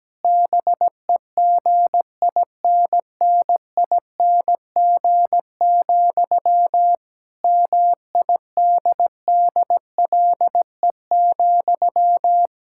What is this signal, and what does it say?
Morse code, heard as a keyed tone, about 17 wpm, 700 Hz. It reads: BEGINNING, MIDDLE,
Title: Sentences from Top 1000 Words - Review Part 6 - 17wpm